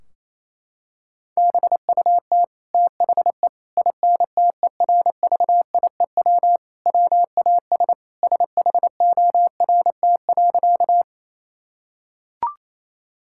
Morse code, heard as a keyed tone, about 28 wpm, 700 Hz. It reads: BUT THE INTERVIEW WAS SHORT.
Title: Sentences from Top 1600 Words 28wpm